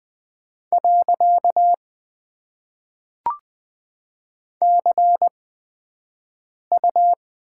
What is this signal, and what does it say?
Morse code, heard as a keyed tone, about 20 wpm, spaced as for 5 wpm, 700 Hz. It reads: . CU
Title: Top 90 QSO Elements - 8x Character Spacing - 20wpm